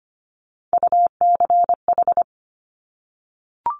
Morse code, heard as a keyed tone, about 25 wpm, 700 Hz. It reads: UCH
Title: Top Three Letter Combinations in Words 25wpm